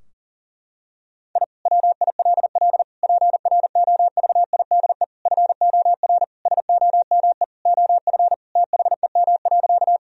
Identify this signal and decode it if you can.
Morse code, heard as a keyed tone, about 40 wpm, 700 Hz. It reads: I WILL PROVIDE FOR SOME OF THEM.
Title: Sentences from Top 1000 Words - Review Part 3 - 40wpm